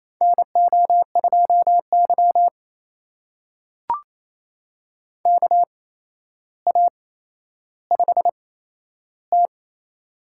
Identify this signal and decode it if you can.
Morse code, heard as a keyed tone, about 28 wpm, spaced as for 12 wpm, 700 Hz. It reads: NO2Y KA5T
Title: USA Call Signs - 4 Characters - 8x Character Spacing - 28wpm